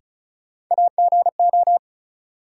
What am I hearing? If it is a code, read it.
Morse code, heard as a keyed tone, about 36 wpm, 700 Hz. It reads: AGO
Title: Sentences from Top 500 Words - New Word Drill - 35wpm